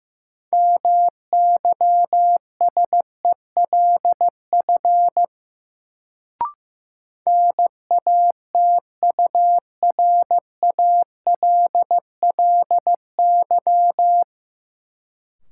Morse code, heard as a keyed tone, about 16 wpm, 700 Hz. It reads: MYSELF NATURALLY
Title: Sentences from Top 1300 Words - New Word Drill Alt - 15wpm